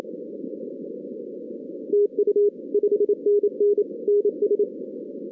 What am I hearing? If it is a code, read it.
Morse code, heard as a keyed tone, about 26 wpm, 440 Hz. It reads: TU 5NN NS
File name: 2016 arrl dx cw / 2016-ARRL-DX-CW-K5TR- - 0-41